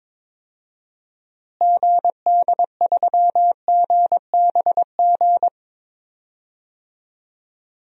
Morse code, heard as a keyed tone, about 22 wpm, 700 Hz. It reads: GD3GBG
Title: DXCC-100 Call Signs - 6 Characters - Repeated 3x - 22wpm